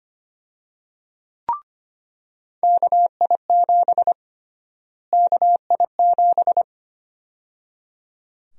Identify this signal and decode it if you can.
Morse code, heard as a keyed tone, about 25 wpm, 700 Hz. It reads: KI7 KI7
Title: USA Call Sign Prefixes - Repeated 2x - 25wpm